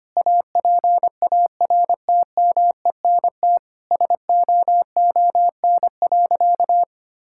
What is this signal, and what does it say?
Morse code, heard as a keyed tone, about 25 wpm, 700 Hz. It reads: APARTMENT SOON.
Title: Sentences from Top 1000 Words - Review Part 6 - 25wpm